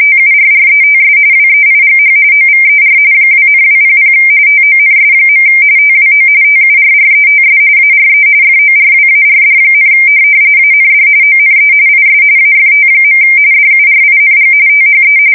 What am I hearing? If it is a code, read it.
RTTY, about 45.45 baud. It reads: TURRETS WEIGH OVER 2,200 TONS. TWO TURRETS ARE FORWARD, AND ONE IS AFT. DURING THE
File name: fsk170 / USS-New-Jersey